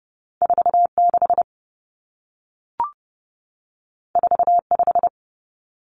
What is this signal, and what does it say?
Morse code, heard as a keyed tone, about 30 wpm, 700 Hz. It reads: 46 45
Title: Common Cut Numbers - 2-Digit 30wpm